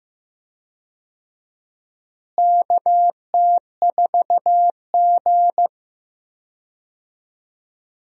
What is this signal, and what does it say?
Morse code, heard as a keyed tone, about 15 wpm, 700 Hz. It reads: KT4G
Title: USA Call Signs - 4 Characters - Repeated 3x - 15wpm